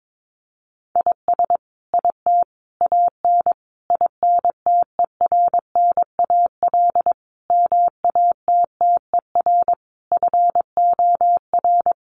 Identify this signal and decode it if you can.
Morse code, heard as a keyed tone, about 22 wpm, 700 Hz. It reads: IS IT AN INTERNAL MATTER FOR
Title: Sentences from Top 1400 Words 22wpm